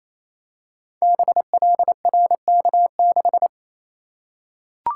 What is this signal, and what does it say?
Morse code, heard as a keyed tone, about 28 wpm, 700 Hz. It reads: BLRK6
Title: Code Groups - 5 Letter:Numbers - 28wpm